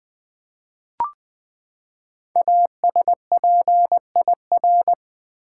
Morse code, heard as a keyed tone, about 20 wpm, 700 Hz. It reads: ASPIR
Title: Build A Word - 5 Letters At A Time - 10-Letter Words - 20wpm